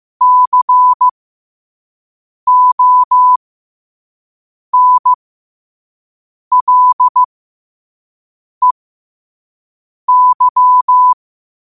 Morse code, heard as a keyed tone, about 15 wpm, spaced as for 5 wpm, 1000 Hz. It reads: CONLEY
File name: qotdfiles / 5wpm